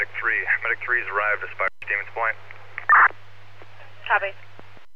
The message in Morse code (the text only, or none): none